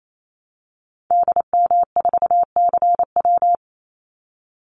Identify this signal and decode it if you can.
Morse code, heard as a keyed tone, about 28 wpm, 700 Hz. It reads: DM4CW
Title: Call Signs - 28wpm